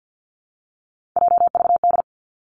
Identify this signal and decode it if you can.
Morse code, heard as a keyed tone, about 50 wpm, 700 Hz. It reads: JVD
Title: Three Challenging Letter-Number Combinations 50wpm